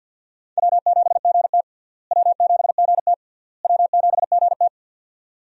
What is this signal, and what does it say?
Morse code, heard as a keyed tone, about 51 wpm, 700 Hz. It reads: W7GT W7GT W7GT
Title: USA Call Signs - 4 Characters - Repeated 3x - 50wpm